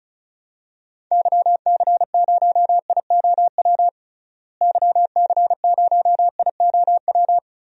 Morse code, heard as a keyed tone, about 36 wpm, 700 Hz. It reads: YC0IOW YC0IOW
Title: DXCC-100 Call Signs - 6 Characters - Repeated 3x - 35wpm